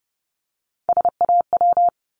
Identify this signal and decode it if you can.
Morse code, heard as a keyed tone, about 32 wpm, 700 Hz. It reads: SAW
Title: Multi-letter Phonemes Repeated 3x 30wpm